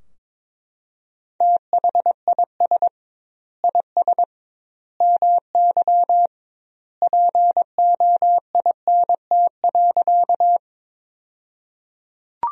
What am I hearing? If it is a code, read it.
Morse code, heard as a keyed tone, about 22 wpm, 700 Hz. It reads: THIS IS MY POINT.
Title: 2x Word-Spacing - Sentences from Top 200 Words - 22wpm